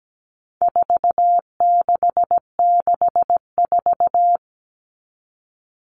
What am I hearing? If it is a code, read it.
Morse code, heard as a keyed tone, about 17 wpm, 700 Hz. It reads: 4664
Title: All Cut Numbers - 4-Digit 17wpm